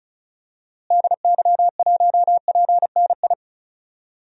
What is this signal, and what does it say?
Morse code, heard as a keyed tone, about 36 wpm, 700 Hz. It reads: DY1PNI
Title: DXCC-100 Call Signs - 6 Characters - Repeated 2x - 35wpm